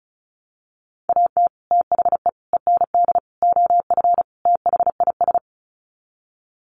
Morse code, heard as a keyed tone, about 34 wpm, 700 Hz. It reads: AT THE END OF THIS
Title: Sets of 5 Words 35wpm